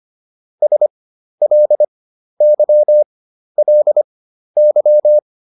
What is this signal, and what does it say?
Morse code, heard as a keyed tone, about 25 wpm, spaced as for 12 wpm, 600 Hz. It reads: SLYLY